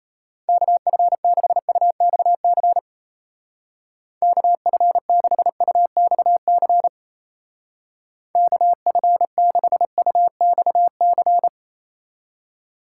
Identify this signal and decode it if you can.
Morse code, heard as a keyed tone, about 32 wpm, 700 Hz. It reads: KF6UXC KF6UXC KF6UXC
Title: Speed Racing - USA Call Signs 28wpm